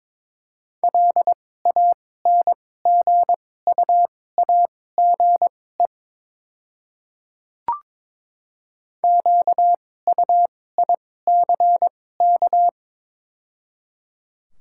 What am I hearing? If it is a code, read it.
Morse code, heard as a keyed tone, about 22 wpm, spaced as for 16 wpm, 690 Hz. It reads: LANGUAGE QUICK
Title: 2x Character Spacing - Top 500 Words - 22wpm